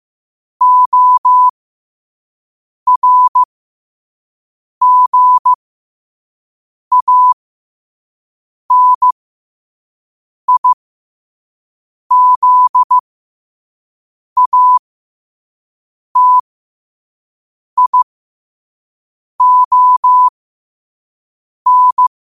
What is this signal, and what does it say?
Morse code, heard as a keyed tone, about 15 wpm, spaced as for 5 wpm, 1000 Hz. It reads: ORGANIZATION
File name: qotdfiles / 5wpm